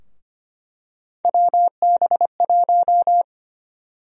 Morse code, heard as a keyed tone, about 25 wpm, 700 Hz. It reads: WB1
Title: USA Call Sign Prefixes - Repeated 2x - 25wpm